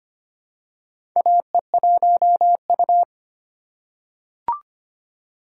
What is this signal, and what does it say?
Morse code, heard as a keyed tone, about 25 wpm, 700 Hz. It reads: AE1U
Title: USA Call Signs - 4 Characters - 3x Character Spacing - 25wpm